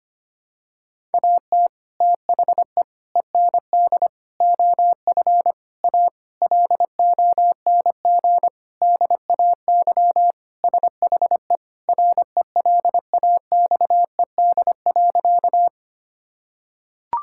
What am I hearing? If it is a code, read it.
Morse code, heard as a keyed tone, about 25 wpm, 700 Hz. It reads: AT THE END OF A LONG DAY SHE RELAXED.
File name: practice / 25wpm / Sentences with 4-Word Phrases - 25wpm